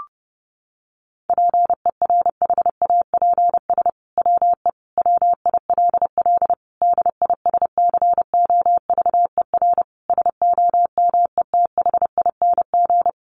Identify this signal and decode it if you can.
Morse code, heard as a keyed tone, about 30 wpm, 700 Hz. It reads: PERHAPS WE WILL DISCOVER SOMETHING
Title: Sentences from Top 1100 Words 30wpm